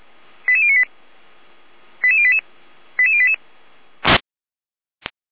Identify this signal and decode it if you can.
Five-tone selective call (ZVEI), heard as quick runs of five tones, tones of about 80 ms. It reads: 88098 88088 88088